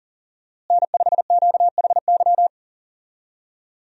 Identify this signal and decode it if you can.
Morse code, heard as a keyed tone, about 40 wpm, 700 Hz. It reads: N5QHY
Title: USA Call Signs - 5 Characters - Repeated 3x - 40wpm